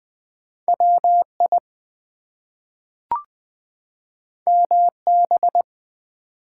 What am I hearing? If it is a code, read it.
Morse code, heard as a keyed tone, about 20 wpm, 700 Hz. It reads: WI MB
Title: US States and Canadian Provinces 20wpm